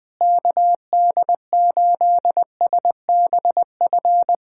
Morse code, heard as a keyed tone, about 21 wpm, 700 Hz. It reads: KD8SBF
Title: USA Callsign DE Callsign - 6 Characters - 20wpm